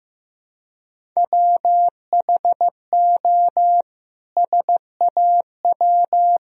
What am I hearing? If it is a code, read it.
Morse code, heard as a keyed tone, about 15 wpm, 700 Hz. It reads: WHO SAW
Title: Sentences from Top 1000 Words - Review Part 1 - 15wpm